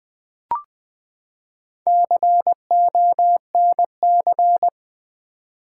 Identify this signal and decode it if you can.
Morse code, heard as a keyed tone, about 20 wpm, 700 Hz. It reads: CONC
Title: Build A Word - 4 Letters At A Time - 10-Letter Words - 20wpm